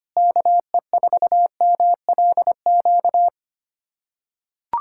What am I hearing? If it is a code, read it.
Morse code, heard as a keyed tone, about 25 wpm, 700 Hz. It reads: KE4MLQ
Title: USA Callsign DE Callsign - 6 Characters - 25wpm